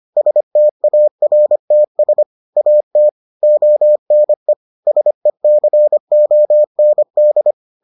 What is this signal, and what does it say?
Morse code, heard as a keyed tone, about 25 wpm, 600 Hz. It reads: STARTS AT ONE SECOND